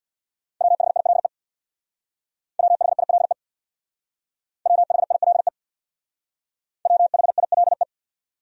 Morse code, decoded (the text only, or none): WHILE WHILE WHILE WHILE